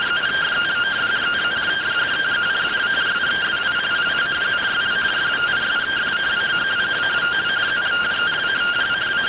Two-tone teleprinter signal, about 170 Hz shift.